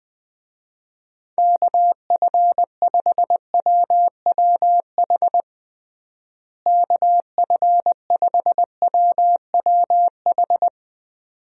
Morse code, heard as a keyed tone, about 21 wpm, 700 Hz. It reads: KF5WWH KF5WWH
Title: USA Call Signs - 6 Characters - Repeated 2x - 20wpm